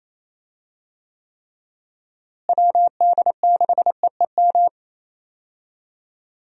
Morse code, heard as a keyed tone, about 28 wpm, 700 Hz. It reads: WD6EEM
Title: USA Call Signs - 6 Characters - Repeated 2x - 28wpm